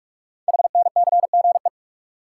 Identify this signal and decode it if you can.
Morse code, heard as a keyed tone, about 44 wpm, 700 Hz. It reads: HNCGE
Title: Code Groups - 5 Letters - 45wpm